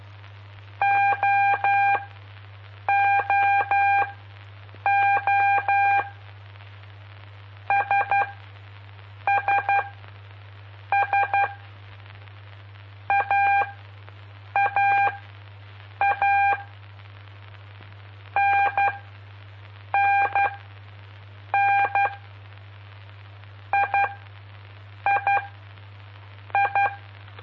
Morse code, heard as a keyed tone, about 12 wpm, spaced as for 6 wpm, 800 Hz. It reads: OOO SSS AAA NNN III
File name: pp5bar / aulas de telegrafia / Aula-05